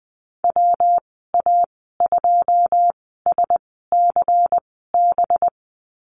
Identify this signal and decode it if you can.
Morse code, heard as a keyed tone, about 20 wpm, spaced as for 15 wpm, 700 Hz. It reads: WA2SCB
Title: 200 Lesson 32g - Letter k - Call Signs 20wpm